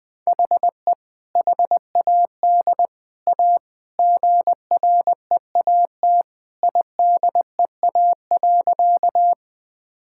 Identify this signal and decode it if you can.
Morse code, heard as a keyed tone, about 21 wpm, 700 Hz. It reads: HE HAD A GREAT IDEA.
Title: Sentences with 3-Word Phrases - 20wpm